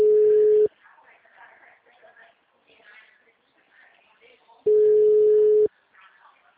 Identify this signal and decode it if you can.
Ringback tone (Europe, 425 Hz), with speech.
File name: out- -2006- - - .137